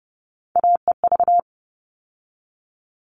Morse code, heard as a keyed tone, about 30 wpm, 700 Hz. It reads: AEV
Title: All Cut Numbers - 3-Digit 30wpm